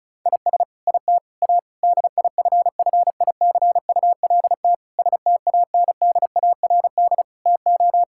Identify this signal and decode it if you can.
Morse code, heard as a keyed tone, about 36 wpm, 700 Hz. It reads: IS IT A DIFFICULT STANDARD TO
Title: Sentences from Top 1000 Words - Review Part 6 - 35wpm